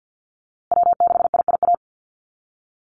Morse code, heard as a keyed tone, about 50 wpm, 700 Hz. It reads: W6IIA
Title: USA Call Signs Repeated 2x 50wpm